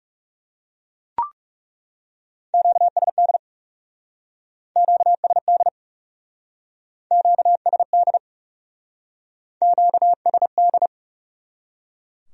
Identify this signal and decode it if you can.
Morse code, heard as a keyed tone, about 36 wpm, 700 Hz. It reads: QSD QSD QSD QSD
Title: Speed-Racing Q-Codes - 30wpm